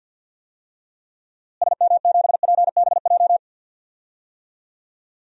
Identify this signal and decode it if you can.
Morse code, heard as a keyed tone, about 53 wpm, 700 Hz. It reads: SM7PBJ
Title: DXCC-100 Call Signs - 6 Characters - Repeated 2x - 50wpm